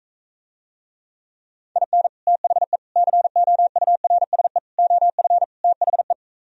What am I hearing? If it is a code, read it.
Morse code, heard as a keyed tone, about 45 wpm, 700 Hz. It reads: IN THE COURSE OF THE